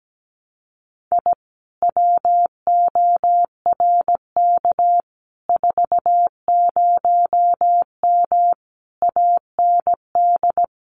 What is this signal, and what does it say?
Morse code, heard as a keyed tone, about 17 wpm, 700 Hz. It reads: I WORK 40M AND
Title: Common QSO Phrases Encore 17wpm